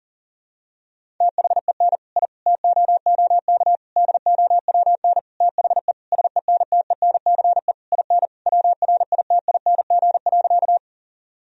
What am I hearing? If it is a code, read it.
Morse code, heard as a keyed tone, about 40 wpm, 700 Hz. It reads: THEN I TOOK DOWN THE SENTENCE IN WRITING.
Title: Sentences from Top 1000 Words - Review Part 1 - 40wpm